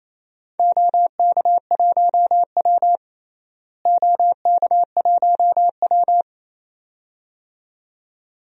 Morse code, heard as a keyed tone, about 29 wpm, 700 Hz. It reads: OK1W OK1W
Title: DXCC-100 Call Signs - 4 Characters - Repeated 2x - 28wpm